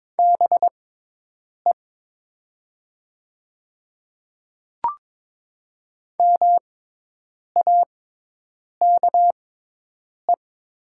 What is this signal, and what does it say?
Morse code, heard as a keyed tone, about 21 wpm, spaced as for 8 wpm, 700 Hz. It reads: BE MAKE
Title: 6x Character Spacing - Top 300 Words - 22wpm